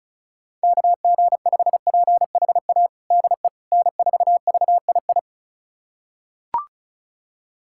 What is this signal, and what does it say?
Morse code, heard as a keyed tone, about 36 wpm, 700 Hz. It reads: KG5PHA DE N4VII
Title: USA Callsign DE Callsign 35wpm